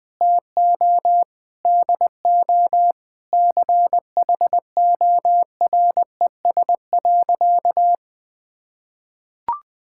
Morse code, heard as a keyed tone, about 20 wpm, 700 Hz. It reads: TO DO CHORES.
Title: Sentences with 5-Word Phrases - 20wpm